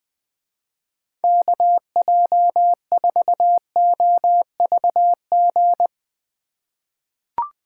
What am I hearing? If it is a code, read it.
Morse code, heard as a keyed tone, about 20 wpm, 700 Hz. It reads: KJ4OVG
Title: USA Call Signs - 2x Character Spacing - 20wpm